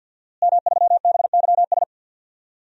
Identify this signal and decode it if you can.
Morse code, heard as a keyed tone, about 51 wpm, 700 Hz. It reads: M3BYS
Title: DXCC-100 Call Signs - 5 Characters - Repeated 2x - 50wpm